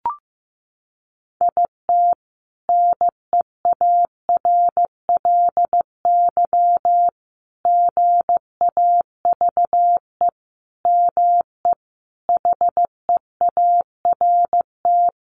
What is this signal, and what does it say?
Morse code, heard as a keyed tone, about 15 wpm, 700 Hz. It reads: IT NEARLY GAVE ME HEART
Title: Sentences from Top 1300 Words 15wpm